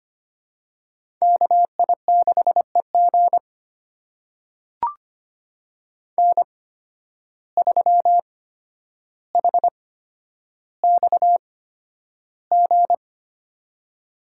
Morse code, heard as a keyed tone, about 25 wpm, spaced as for 11 wpm, 700 Hz. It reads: KI6EG N3HXG
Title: USA Call Signs - 5 Characters - 8x Character Spacing - 25wpm